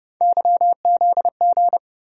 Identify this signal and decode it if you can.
Morse code, heard as a keyed tone, about 30 wpm, 700 Hz. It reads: YZG
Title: USA Call Sign Postfixes - 3 Letters 30wpm